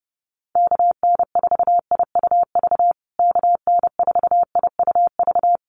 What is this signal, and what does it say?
Morse code, heard as a keyed tone, about 30 wpm, 700 Hz. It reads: KN4IUV KN4IUV
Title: USA Call Signs Repeated 3x 30wpm